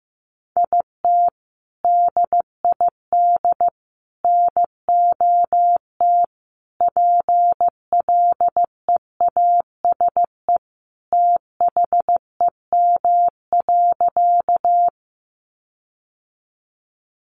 Morse code, heard as a keyed tone, about 15 wpm, 700 Hz. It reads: IT DID NOT PLEASE THEM.
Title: Sentences from Top 1500 Words 15wpm